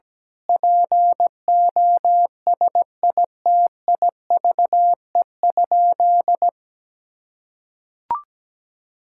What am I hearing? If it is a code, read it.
Morse code, heard as a keyed tone, about 17 wpm, 700 Hz. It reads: POSITIVE?
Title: Sentences from Top 1000 Words - Review Part 5 - 17wpm